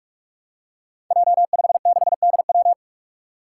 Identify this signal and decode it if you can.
Morse code, heard as a keyed tone, about 44 wpm, 700 Hz. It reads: J56DW
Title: Five Challenging Letter-Number Combinations 45wpm